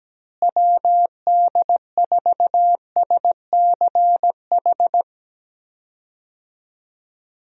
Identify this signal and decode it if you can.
Morse code, heard as a keyed tone, about 17 wpm, 700 Hz. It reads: WD4SCH
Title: USA Call Signs - 6 Characters - Repeated 2x - 17wpm